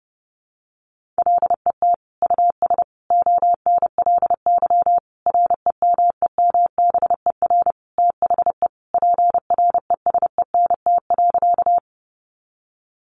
Morse code, heard as a keyed tone, about 30 wpm, 700 Hz. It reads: LET US ONLY REMEMBER THE PRESENT.
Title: Sentences from Top 800 Words 30wpm